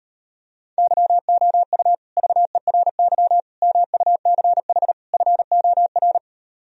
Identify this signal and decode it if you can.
Morse code, heard as a keyed tone, about 38 wpm, 700 Hz. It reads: YOU VERY MUCH FOR